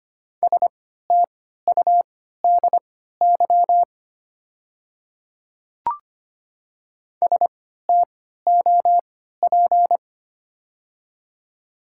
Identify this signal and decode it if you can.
Morse code, heard as a keyed tone, about 25 wpm, spaced as for 14 wpm, 700 Hz. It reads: STUDY STOP
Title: 3x Character Spacing - Top 300 Words - 25wpm